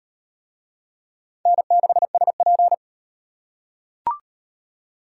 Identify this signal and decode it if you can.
Morse code, heard as a keyed tone, about 38 wpm, 700 Hz. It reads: N6SP